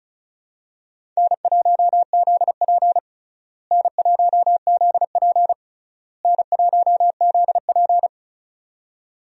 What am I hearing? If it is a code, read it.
Morse code, heard as a keyed tone, about 36 wpm, 700 Hz. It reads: N1ZP N1ZP N1ZP
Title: USA Call Signs - 4 Characters - Repeated 3x - 35wpm